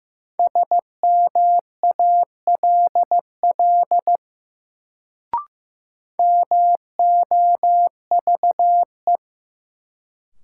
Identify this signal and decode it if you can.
Morse code, heard as a keyed tone, about 15 wpm, 700 Hz. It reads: SMALL MOVE
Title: Sentences from Top 200 Words - New Word Drill - 15wpm